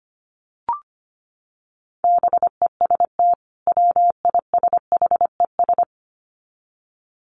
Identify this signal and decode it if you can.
Morse code, heard as a keyed tone, about 25 wpm, 700 Hz. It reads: BEST WISHES
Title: Common QSO Phrases 25wpm